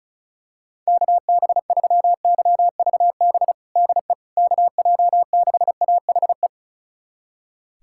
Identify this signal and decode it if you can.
Morse code, heard as a keyed tone, about 36 wpm, 700 Hz. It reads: KB3YVB DE KJ6AHE
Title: USA Callsign DE Callsign - 6 Characters - 35wpm